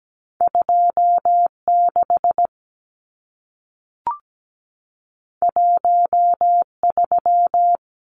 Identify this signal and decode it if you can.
Morse code, heard as a keyed tone, about 17 wpm, 700 Hz. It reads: 26 13
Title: Numbers - 2-Digit 17wpm